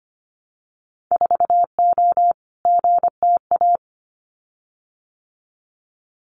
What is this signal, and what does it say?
Morse code, heard as a keyed tone, about 25 wpm, 700 Hz. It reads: 4O GTA
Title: Winter Field Day Exchanges 25wpm